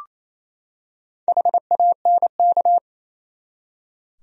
Morse code, heard as a keyed tone, about 28 wpm, 700 Hz. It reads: HANK
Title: USA Common Names - 28wpm